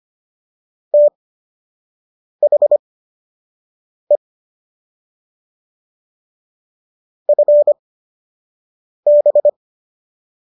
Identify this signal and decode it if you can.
Morse code, heard as a keyed tone, about 25 wpm, spaced as for 6 wpm, 590 Hz. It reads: THE FB